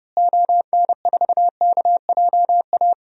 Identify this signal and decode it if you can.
Morse code, heard as a keyed tone, about 30 wpm, 700 Hz. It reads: ON4KJA
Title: DXCC-100 Call Signs - 6 Characters - Repeated 2x - 30wpm